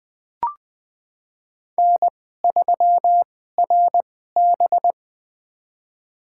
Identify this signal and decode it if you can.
Morse code, heard as a keyed tone, about 20 wpm, spaced as for 14 wpm, 700 Hz. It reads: N3RB
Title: USA Call Signs - 4 Characters - 2x Character Spacing - 20wpm